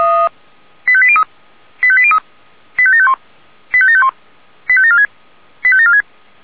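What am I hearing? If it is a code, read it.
Five-tone selective call (ZVEI), heard as quick runs of five tones, tones of about 70 ms. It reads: 85803 85803 85751 85751 85746 85746
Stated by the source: ZVEI five-tone call 85827 received by SDR.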